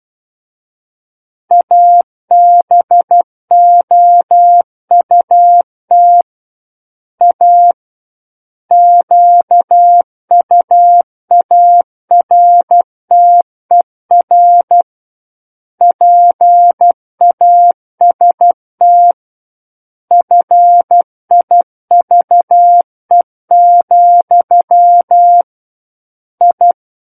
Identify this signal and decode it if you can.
Morse code, heard as a keyed tone, about 12 wpm, 700 Hz. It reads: ABOUT A QUARTER PAST FIVE, I
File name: Book One / 03-Chapter 3 - 12 WPM